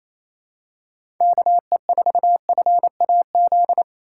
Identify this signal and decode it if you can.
Morse code, heard as a keyed tone, about 28 wpm, 700 Hz. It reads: KE4FAZ
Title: USA Call Signs - 6 Characters - 4x Character Spacing - 28wpm